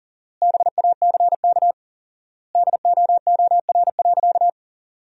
Morse code, heard as a keyed tone, about 40 wpm, 700 Hz. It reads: BACK DOOR.
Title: 4x Word-Spacing - Sentences from Top 300 Words - 40wpm